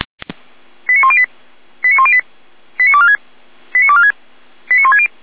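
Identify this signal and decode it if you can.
Five-tone selective call (ZVEI), heard as quick runs of five tones, tones of about 70 ms. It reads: 89108 89108 89246 89246 89160
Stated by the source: ZVEI five-tone selective call received by SDR.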